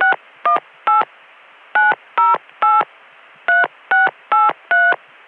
DTMF, heard as telephone touch tones, tones of about 160 ms. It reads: 618 908 3683